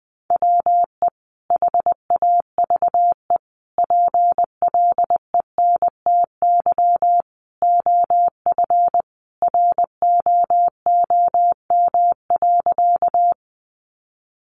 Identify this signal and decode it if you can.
Morse code, heard as a keyed tone, about 20 wpm, 700 Hz. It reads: WE HAVE PLENTY OF ROOM.
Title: Sentences from Top 1300 Words 20wpm